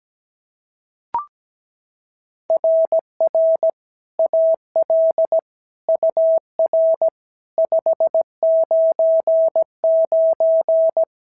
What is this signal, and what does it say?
Morse code, heard as a keyed tone, about 16 wpm, 650 Hz. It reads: RR AL UR 599
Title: SOTA QSO Practice 17wpm